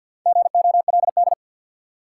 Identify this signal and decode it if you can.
Morse code, heard as a keyed tone, about 53 wpm, 700 Hz. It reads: GOLD
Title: Four Letter Words - 50wpm